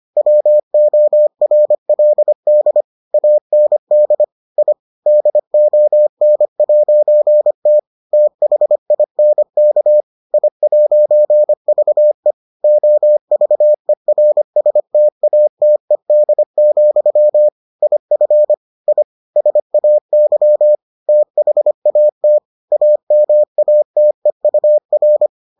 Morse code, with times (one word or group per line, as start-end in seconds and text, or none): 0.2-2.8 WORLD
3.1-4.2 AND
4.6-4.7 I
5.1-7.8 DON'T
8.1-10.0 THINK
10.3-12.3 I'VE
12.6-17.5 OVERSTATED,
17.8-18.5 IF
18.9-19.0 I
19.4-20.8 SAY
21.1-22.4 THAT
22.7-25.3 AMATEUR